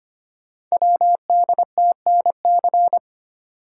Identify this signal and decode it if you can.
Morse code, heard as a keyed tone, about 25 wpm, 700 Hz. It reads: WDTNC
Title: Code Groups - 5 Letters - 25wpm